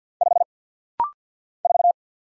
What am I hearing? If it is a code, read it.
Morse code, heard as a keyed tone, about 50 wpm, 690 Hz. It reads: 5 4
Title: Single Letter-Number Alt - Rapid-Fire 50wpm